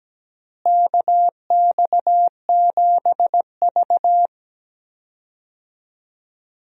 Morse code, heard as a keyed tone, about 17 wpm, 700 Hz. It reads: KX7V
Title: USA Call Signs - 4 Characters - Repeated 3x - 17wpm